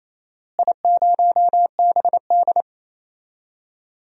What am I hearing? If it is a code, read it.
Morse code, heard as a keyed tone, about 27 wpm, 700 Hz. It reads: I0BD
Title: DXCC-100 Call Signs - 4 Characters - Repeated 2x - 28wpm